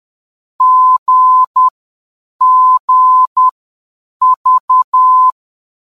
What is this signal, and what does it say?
Morse code, heard as a keyed tone, about 10 wpm, spaced as for 7 wpm, 1020 Hz. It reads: GGV
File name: GGV-VOR DME